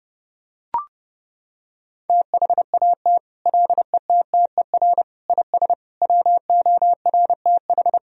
Morse code, heard as a keyed tone, about 30 wpm, 700 Hz. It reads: THAT LETTER IS WORTH
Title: Sentences from Top 1000 Words - Review Part 3 - 30wpm